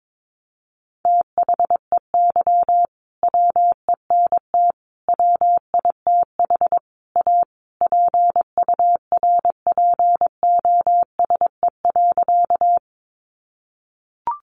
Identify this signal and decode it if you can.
Morse code, heard as a keyed tone, about 22 wpm, 700 Hz. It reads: THEY WENT WITH A PURPOSE.
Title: Sentences from Top 800 Words 22wpm